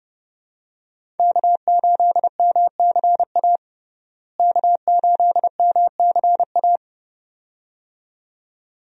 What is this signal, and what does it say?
Morse code, heard as a keyed tone, about 29 wpm, 700 Hz. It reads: K8MCA K8MCA
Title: USA Call Signs - 5 Characters - Repeated 2x - 30wpm